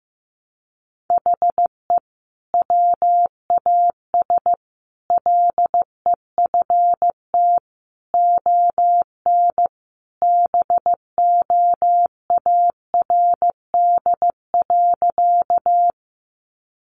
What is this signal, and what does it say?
Morse code, heard as a keyed tone, about 15 wpm, 700 Hz. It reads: HE WAS LEFT ON BOARD.
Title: Sentences from Top 700 Words 15wpm